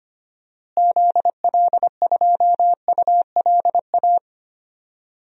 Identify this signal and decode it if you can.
Morse code, heard as a keyed tone, about 25 wpm, 700 Hz. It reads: ZL2ULA
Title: DXCC-100 Call Signs - 6 Characters - Repeated 2x - 25wpm